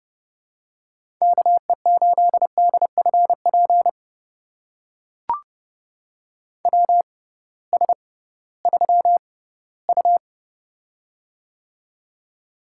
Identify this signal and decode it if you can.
Morse code, heard as a keyed tone, about 30 wpm, spaced as for 18 wpm, 700 Hz. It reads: KE8DFP WS3U
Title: USA Call Signs - 6x Character Spacing - 30wpm